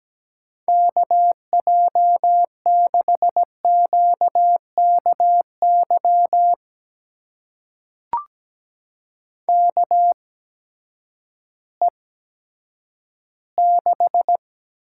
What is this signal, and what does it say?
Morse code, heard as a keyed tone, about 17 wpm, spaced as for 10 wpm, 700 Hz. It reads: KJ6QKY KE6
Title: USA Call Signs - 6 Characters - 8x Character Spacing - 17wpm